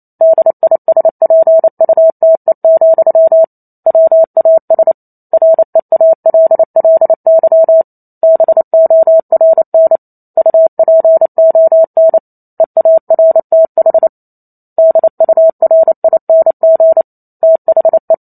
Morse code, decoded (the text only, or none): DISPUTE, WAS REALLY BORN UPON EARTH DURING THE